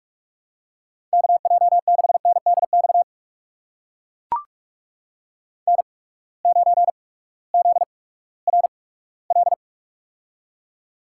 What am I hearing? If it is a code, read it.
Morse code, heard as a keyed tone, about 44 wpm, spaced as for 20 wpm, 700 Hz. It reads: KJ6NDX N9ZRL
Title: USA Call Signs - 8x Character Spacing - 45wpm